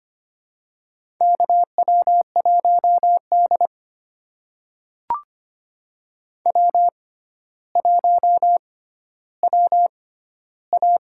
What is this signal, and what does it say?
Morse code, heard as a keyed tone, about 25 wpm, spaced as for 13 wpm, 700 Hz. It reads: KW1D W1WA
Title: USA Call Signs - 4 Characters - 6x Character Spacing - 25wpm